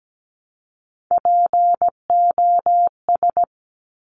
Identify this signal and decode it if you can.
Morse code, heard as a keyed tone, about 17 wpm, 700 Hz. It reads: POS
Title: Top Three Letter Combinations in Words 17wpm